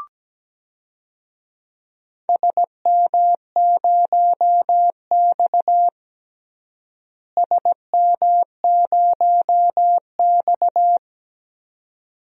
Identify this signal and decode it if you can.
Morse code, heard as a keyed tone, about 17 wpm, 700 Hz. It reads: SM0X SM0X
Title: DXCC-100 Call Signs - 4 Characters - Repeated 2x - 17wpm